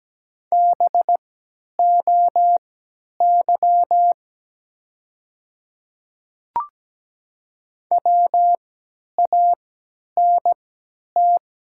Morse code, heard as a keyed tone, about 17 wpm, spaced as for 10 wpm, 700 Hz. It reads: BOY WANT
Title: 3x Character Spacing - Top 200 Words - 17wpm